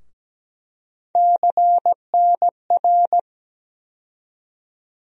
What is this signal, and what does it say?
Morse code, heard as a keyed tone, about 17 wpm, 700 Hz. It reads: CNR
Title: USA Call Sign Postfixes - 3 Letters 17wpm